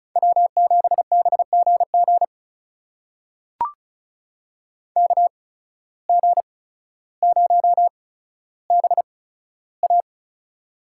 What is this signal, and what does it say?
Morse code, heard as a keyed tone, about 34 wpm, spaced as for 15 wpm, 700 Hz. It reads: W7BGG KG0BA
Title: USA Call Signs - 5 Characters - 8x Character Spacing - 35wpm